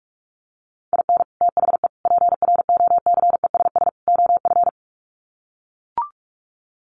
Morse code, heard as a keyed tone, about 44 wpm, 700 Hz. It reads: IN THE PROCESS OF
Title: Sets of 4 Words 45wpm